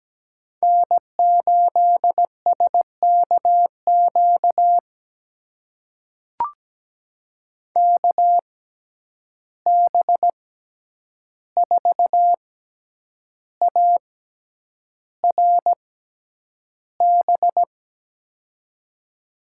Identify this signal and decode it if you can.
Morse code, heard as a keyed tone, about 17 wpm, spaced as for 8 wpm, 700 Hz. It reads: N8SKQ KB4ARB
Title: USA Call Signs - 6x Character Spacing - 17wpm